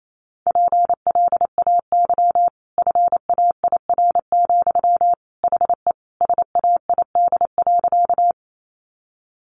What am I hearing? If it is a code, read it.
Morse code, heard as a keyed tone, about 28 wpm, 700 Hz. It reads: PLAY FAIR, HE SAID.
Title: Sentences from Top 1300 Words 28wpm